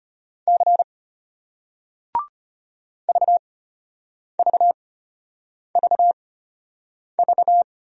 Morse code, heard as a keyed tone, about 33 wpm, spaced as for 8 wpm, 700 Hz. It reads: C VVVV